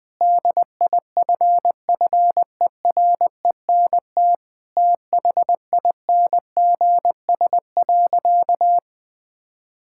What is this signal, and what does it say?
Morse code, heard as a keyed tone, about 20 wpm, 700 Hz. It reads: DIFFERENT THINGS.
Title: Sentences from Top 1000 Words - Review Part 6 - 20wpm